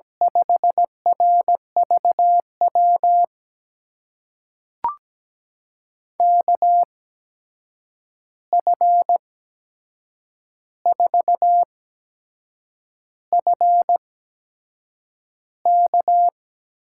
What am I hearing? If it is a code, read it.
Morse code, heard as a keyed tone, about 17 wpm, spaced as for 7 wpm, 700 Hz. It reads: 5RVW KF4FK
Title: USA Call Signs - 6 Characters - 8x Character Spacing - 17wpm